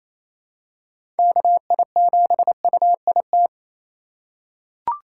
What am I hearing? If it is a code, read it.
Morse code, heard as a keyed tone, about 28 wpm, 700 Hz. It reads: KI7UIT
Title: USA Call Signs - 6 Characters - 2x Character Spacing - 28wpm